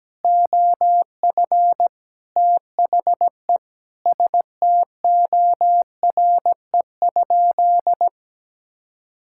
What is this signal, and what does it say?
Morse code, heard as a keyed tone, about 17 wpm, 700 Hz. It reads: OF THE STORE?
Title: Sentences from Top 1000 Words - Review Part 5 - 17wpm